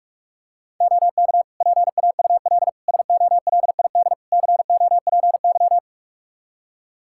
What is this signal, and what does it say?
Morse code, heard as a keyed tone, about 43 wpm, 700 Hz. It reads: OK PAUL SOLID COPY
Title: K1USN SST QSO Practice 45wpm